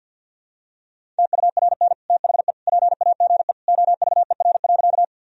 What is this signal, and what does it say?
Morse code, heard as a keyed tone, about 50 wpm, 700 Hz. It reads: TURN THE PAGE OVER.
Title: Sentences from Top 1000 Words - Review Part 1 - 50wpm